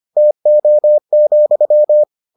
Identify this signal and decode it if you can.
Morse code, heard as a keyed tone, about 25 wpm, 600 Hz. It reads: TO,